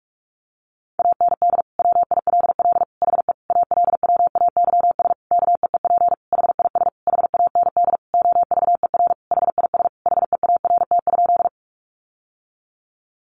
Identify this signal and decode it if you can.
Morse code, heard as a keyed tone, about 44 wpm, 700 Hz. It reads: AND WILL HE ALWAYS KEEP HIS HAND OVER HIS HEART?
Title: Sentences from Top 1200 Words 45wpm